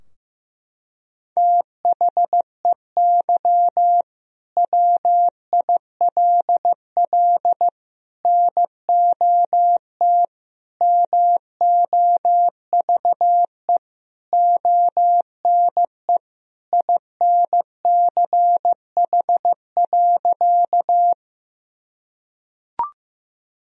Morse code, heard as a keyed tone, about 15 wpm, 700 Hz. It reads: THEY WILL NOT MOVE ONE INCH.
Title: Sentences from Top 1000 Words - Review Part 3 - 15wpm